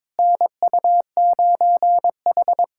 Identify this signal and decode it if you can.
Morse code, heard as a keyed tone, about 22 wpm, 700 Hz. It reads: NU9H
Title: USA Callsign DE Callsign - 4 Characters - 22wpm